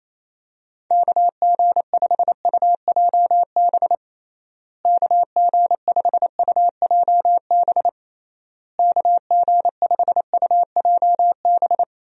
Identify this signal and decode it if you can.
Morse code, heard as a keyed tone, about 28 wpm, 700 Hz. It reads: KG5UJB KG5UJB KG5UJB
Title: USA Call Signs - 6 Characters - Repeated 3x - 28wpm